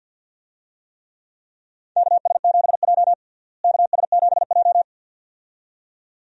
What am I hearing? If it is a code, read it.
Morse code, heard as a keyed tone, about 47 wpm, 700 Hz. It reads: KS7J KS7J
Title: USA Call Signs - 4 Characters - Repeated 2x - 50wpm